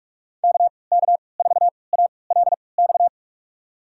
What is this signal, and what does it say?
Morse code, heard as a keyed tone, about 46 wpm, spaced as for 26 wpm, 700 Hz. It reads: KK4ALX
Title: USA Call Signs - 6 Characters - 3x Character Spacing - 45wpm